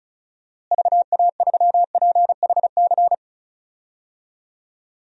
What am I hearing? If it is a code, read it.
Morse code, heard as a keyed tone, about 34 wpm, 700 Hz. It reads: VA3PHC
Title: DXCC-100 Call Signs - 6 Characters - Repeated 3x - 35wpm